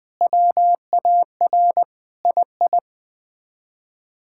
Morse code, heard as a keyed tone, about 20 wpm, 700 Hz. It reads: WAR II
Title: Sets of 6 Words - 20wpm